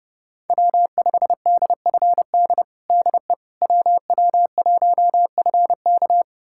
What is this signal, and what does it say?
Morse code, heard as a keyed tone, about 30 wpm, 700 Hz. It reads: W5DFD DE WW1FK
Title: USA Callsign DE Callsign - 5 Characters - 30wpm